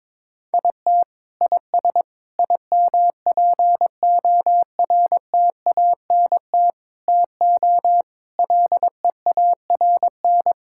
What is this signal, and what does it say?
Morse code, heard as a keyed tone, about 22 wpm, 700 Hz. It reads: IT IS IMPORTANT TO LEARN
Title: Sentences from Top 1000 Words - Review Part 6 - 22wpm